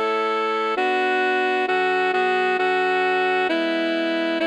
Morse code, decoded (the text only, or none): none